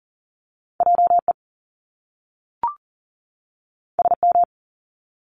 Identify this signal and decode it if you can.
Morse code, heard as a keyed tone, about 40 wpm, 700 Hz. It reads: JE SM
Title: Top Two Letter Combinations in Words 40wpm